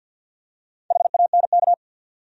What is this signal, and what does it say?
Morse code, heard as a keyed tone, about 47 wpm, 700 Hz. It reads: HANK
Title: USA Common Names - Max 5 Letters - 50wpm